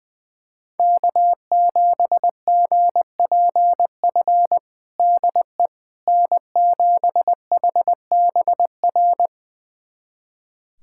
Morse code, heard as a keyed tone, about 21 wpm, 700 Hz. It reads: K7GPF DE N7HBR
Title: USA Callsign DE Callsign - 5 Characters - 20wpm